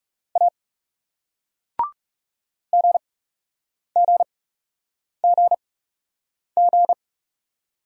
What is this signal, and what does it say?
Morse code, heard as a keyed tone, about 38 wpm, spaced as for 8 wpm, 700 Hz. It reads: A GGGG